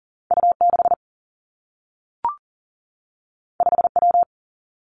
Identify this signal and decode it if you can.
Morse code, heard as a keyed tone, about 39 wpm, 700 Hz. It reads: U 6 5 W
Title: Two Challenging Letter-Number Combinations 40wpm